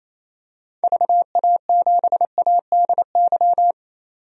Morse code, heard as a keyed tone, about 28 wpm, 700 Hz. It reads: VA7ADY
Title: DXCC-100 Call Signs - 6 Characters - 28wpm